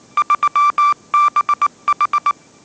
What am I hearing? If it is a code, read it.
Morse code, heard as a keyed tone, about 19 wpm, 1180 Hz. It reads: 3BH
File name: mrx083 20wpm